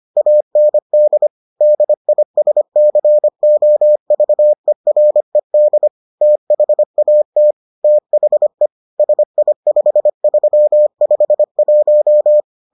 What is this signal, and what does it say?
Morse code, heard as a keyed tone, about 25 wpm, 600 Hz. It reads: AND DISCOVERED THAT THE SI5351